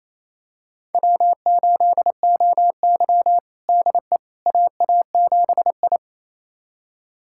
Morse code, heard as a keyed tone, about 28 wpm, 700 Hz. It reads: W8OY DE AA7I
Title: USA Callsign DE Callsign - 4 Characters - 28wpm